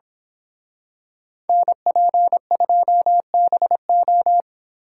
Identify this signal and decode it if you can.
Morse code, heard as a keyed tone, about 26 wpm, 700 Hz. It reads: NP2BO
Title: Speed Racing - USA 2x2 Call Signs 22wpm